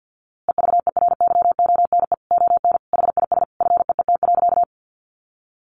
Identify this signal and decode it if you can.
Morse code, heard as a keyed tone, about 50 wpm, 700 Hz. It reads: EVERYONE ON HIS FEET.
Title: A Sound Of Thunder 50wpm